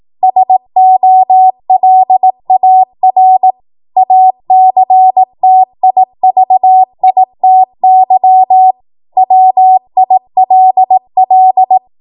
Morse code, heard as a keyed tone, about 18 wpm, 750 Hz. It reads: SOLAR ACTIVITY WILL
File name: Morse / 18WPM